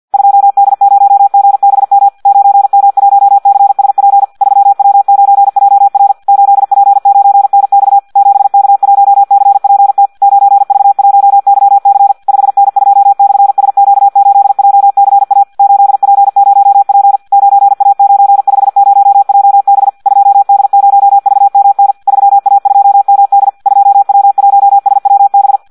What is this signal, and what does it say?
Morse code, heard as a keyed tone, about 50 wpm, 800 Hz, with radio noise. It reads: 2D0GBM 9M1CSP 3W9JR 8P9NX 7K1CPT 9V1YC 5N2/SQ8JCA 8P0P 9A950JB 2D0VMN 3A2MD 2W1SWB
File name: Audio practica CW / qrq-tx-50 2